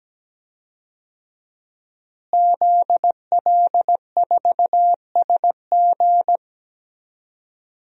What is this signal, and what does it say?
Morse code, heard as a keyed tone, about 17 wpm, 700 Hz. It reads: ZL4SG
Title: DXCC-100 Call Signs - 5 Characters - Repeated 3x - 17wpm